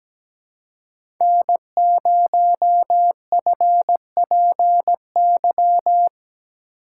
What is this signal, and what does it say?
Morse code, heard as a keyed tone, about 17 wpm, 700 Hz. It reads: N0FPY
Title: USA Call Signs - 5 Characters - 2x Character Spacing - 17wpm